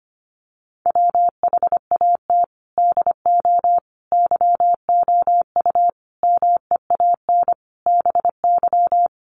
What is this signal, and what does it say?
Morse code, heard as a keyed tone, about 25 wpm, 700 Hz. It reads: WHAT DO YOU MEAN BY
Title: Sentences from Top 200 Words 25wpm